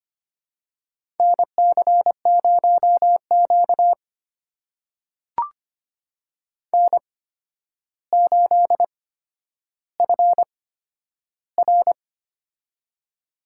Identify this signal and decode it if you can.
Morse code, heard as a keyed tone, about 25 wpm, spaced as for 11 wpm, 700 Hz. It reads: NC0Q N8FR
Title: USA Call Signs - 4 Characters - 8x Character Spacing - 25wpm